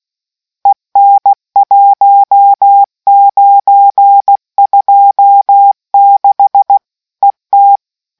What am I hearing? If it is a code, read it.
Morse code, heard as a keyed tone, about 16 wpm, 800 Hz. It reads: EN 1926 ET